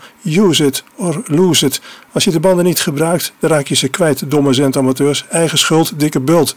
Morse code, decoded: none